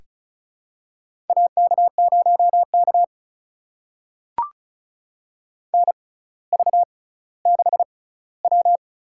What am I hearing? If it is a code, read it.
Morse code, heard as a keyed tone, about 36 wpm, spaced as for 18 wpm, 700 Hz. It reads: AK0K NV6W
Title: USA Call Signs - 4 Characters - 6x Character Spacing - 35wpm